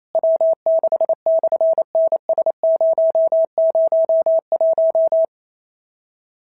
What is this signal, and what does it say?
Morse code, heard as a keyed tone, about 28 wpm, 650 Hz. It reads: W6/NS001
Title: SOTA QSO Practice 28wpm